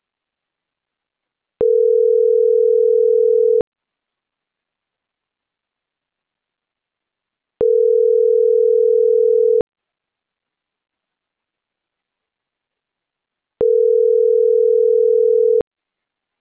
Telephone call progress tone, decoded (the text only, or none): ringback tone